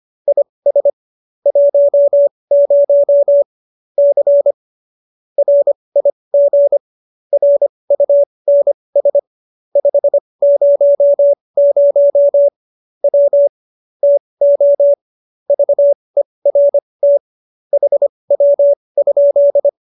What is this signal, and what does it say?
Morse code, heard as a keyed tone, about 25 wpm, spaced as for 20 wpm, 570 Hz. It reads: IS 10 C RIG RUNS 500 W TO VERT HW?
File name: qso215b 20